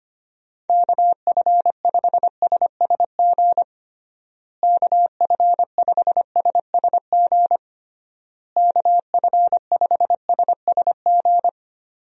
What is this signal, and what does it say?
Morse code, heard as a keyed tone, about 25 wpm, 700 Hz. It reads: KF5SSG KF5SSG KF5SSG
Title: USA Call Signs - 6 Characters - Repeated 3x - 25wpm